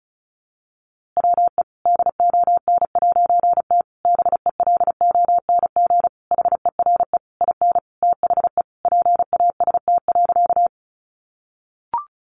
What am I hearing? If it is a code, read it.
Morse code, heard as a keyed tone, about 34 wpm, 700 Hz. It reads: WE DON'T BELONG HERE IN THE PAST.
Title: A Sound Of Thunder 35wpm